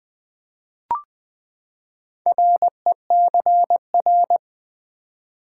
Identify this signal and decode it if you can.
Morse code, heard as a keyed tone, about 20 wpm, 700 Hz. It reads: RECR
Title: Build A Word - 4 Letters At A Time - 11-Letter Words - 20wpm